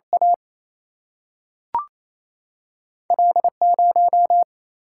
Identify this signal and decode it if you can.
Morse code, heard as a keyed tone, about 28 wpm, 700 Hz. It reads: A L 0
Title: Single-Letter Single-Number 28wpm